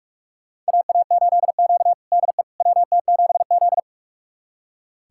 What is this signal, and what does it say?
Morse code, heard as a keyed tone, about 44 wpm, 690 Hz. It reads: AA8Q DE WT7Z
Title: USA Callsign DE Callsign - 4 Characters - 45wpm